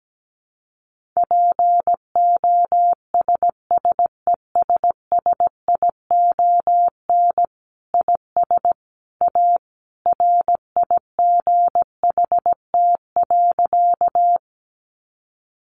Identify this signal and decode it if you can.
Morse code, heard as a keyed tone, about 17 wpm, 700 Hz. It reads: POSSESSION IS A RIGHT.
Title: Sentences from Top 1400 Words 17wpm